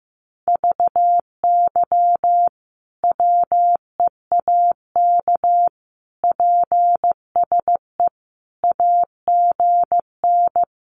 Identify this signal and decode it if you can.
Morse code, heard as a keyed tone, about 15 wpm, 700 Hz. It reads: VY WEAK PSE AGN
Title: Common QSO Phrases 15wpm